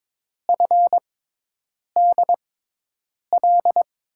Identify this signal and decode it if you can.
Morse code, heard as a keyed tone, about 22 wpm, spaced as for 8 wpm, 700 Hz. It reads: FDL
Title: USA Call Signs - 6 Characters - 6x Character Spacing - 22wpm